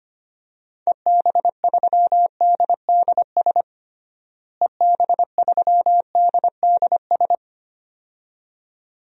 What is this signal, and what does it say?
Morse code, heard as a keyed tone, about 25 wpm, 700 Hz. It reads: EB3DDS EB3DDS
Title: DXCC-100 Call Signs - 6 Characters - Repeated 2x - 25wpm